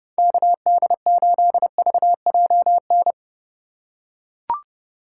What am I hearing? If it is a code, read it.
Morse code, heard as a keyed tone, about 30 wpm, 700 Hz. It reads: KD8VJN
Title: USA Call Signs - 3x Character Spacing - 30wpm